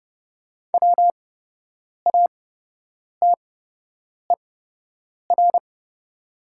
Morse code, heard as a keyed tone, about 30 wpm, spaced as for 8 wpm, 700 Hz. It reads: WATER
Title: 8x Character Spacing - Top 300 Words - 30wpm